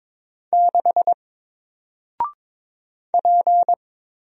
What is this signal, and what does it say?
Morse code, heard as a keyed tone, about 22 wpm, 700 Hz. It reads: 6 P
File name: Single-Letter-Number-22wpm